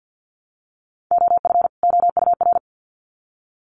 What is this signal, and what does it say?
Morse code, heard as a keyed tone, about 50 wpm, 700 Hz. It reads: OF OUR
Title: Sets of 2 Words 50wpm